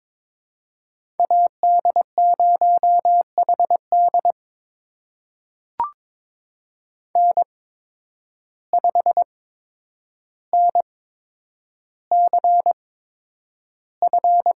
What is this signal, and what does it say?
Morse code, heard as a keyed tone, about 22 wpm, spaced as for 9 wpm, 700 Hz. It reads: AD0HD N5NCF
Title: USA Call Signs - 8x Character Spacing - 22wpm